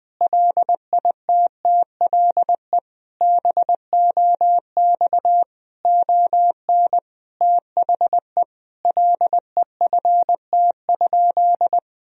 Morse code, heard as a keyed tone, about 20 wpm, 700 Hz. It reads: LITTLE BOX ON THE LEFT?
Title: Sentences from Top 1000 Words - Review Part 2 - 20wpm